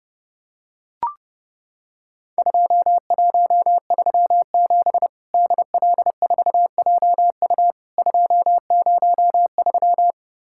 Morse code, heard as a keyed tone, about 32 wpm, 700 Hz. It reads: 2137 DL4JU 203
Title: Worked All Europe DX Contest - QTC Exchange - 30wpm